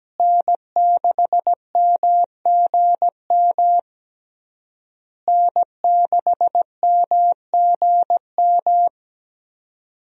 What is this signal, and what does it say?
Morse code, heard as a keyed tone, about 17 wpm, 700 Hz. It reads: N6MGM N6MGM
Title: USA Call Signs - 5 Characters - Repeated 3x - 17wpm